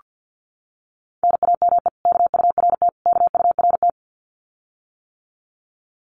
Morse code, heard as a keyed tone, about 48 wpm, 700 Hz. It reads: NAME KURT KURT
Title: Common QSO Phrases Encore 50wpm